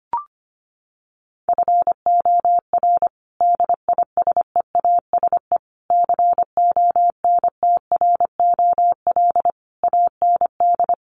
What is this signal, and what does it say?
Morse code, heard as a keyed tone, about 25 wpm, 700 Hz. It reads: FOR DISEASE CONTROL AND
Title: Sets of 5 Words Encore2 25wpm